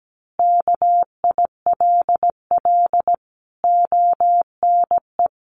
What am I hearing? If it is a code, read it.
Morse code, heard as a keyed tone, about 17 wpm, 700 Hz. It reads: KILL ONE
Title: A Sound Of Thunder 17wpm